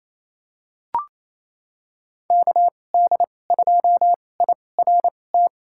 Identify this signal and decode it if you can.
Morse code, heard as a keyed tone, about 28 wpm, spaced as for 20 wpm, 700 Hz. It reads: KD2IRT
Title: USA Call Signs - 2x Character Spacing - 28wpm